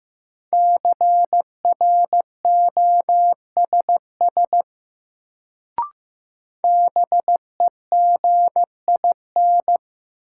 Morse code, heard as a keyed tone, about 15 wpm, 700 Hz. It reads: CROSS BEGIN
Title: Sentences from Top 300 Words - New Word Drill - 15wpm